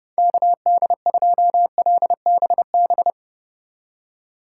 Morse code, heard as a keyed tone, about 36 wpm, 700 Hz. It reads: KD2LBB
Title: Speed Racing - USA 2x3 Call Signs 30wpm